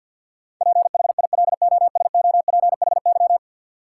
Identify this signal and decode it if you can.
Morse code, heard as a keyed tone, about 53 wpm, 700 Hz. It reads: PHILOSOPHY
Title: Sentences from Top 900 Words - New Word Drill - 50wpm